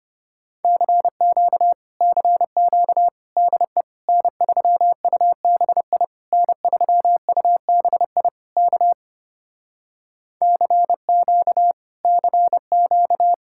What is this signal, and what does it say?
Morse code, heard as a keyed tone, about 30 wpm, 700 Hz. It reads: CQ CQ DE N3UBI N3UBI K CQ CQ
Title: Speed-Racing Common QSO Phrases Encore - 25wpm